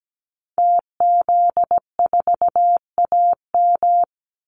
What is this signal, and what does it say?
Morse code, heard as a keyed tone, about 17 wpm, 700 Hz. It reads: TZ4AM
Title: Call Signs - 17wpm